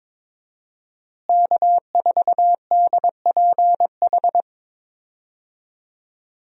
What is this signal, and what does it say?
Morse code, heard as a keyed tone, about 22 wpm, 700 Hz. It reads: K4DPH
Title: USA Call Signs - 5 Characters - Repeated 3x - 22wpm